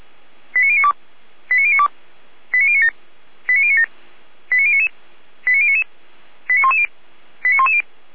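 Five-tone selective call (ZVEI), heard as quick runs of five tones, tones of about 70 ms. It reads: 89092 89092 89097 89097 89099 89099 89110 89110
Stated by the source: ZVEI five-tone call received by SDR.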